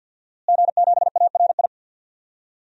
Morse code, heard as a keyed tone, about 52 wpm, 700 Hz. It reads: G7ARI
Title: DXCC-100 Call Signs - 5 Characters - Repeated 3x - 50wpm